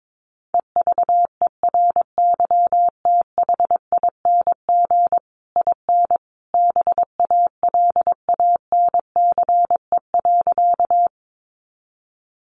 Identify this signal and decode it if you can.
Morse code, heard as a keyed tone, about 22 wpm, 700 Hz. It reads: EVERYTHING IN BALANCE.
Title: A Sound Of Thunder 22wpm